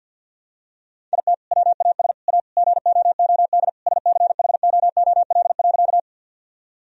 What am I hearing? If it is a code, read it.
Morse code, heard as a keyed tone, about 49 wpm, 700 Hz. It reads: IT WAS A GOOD SCHOOL.
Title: Sentences from Top 1000 Words - Review Part 1 - 50wpm